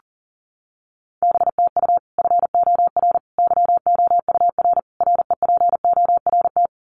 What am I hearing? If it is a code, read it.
Morse code, heard as a keyed tone, about 40 wpm, 700 Hz. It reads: BTU FOR YOUR REPORT
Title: Common QSO Phrases 40wpm